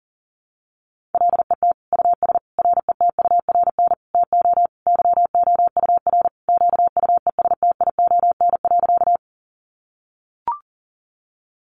Morse code, heard as a keyed tone, about 40 wpm, 700 Hz. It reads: LET US RETURN TO YOUR QUESTION.
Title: Sentences from Top 800 Words 40wpm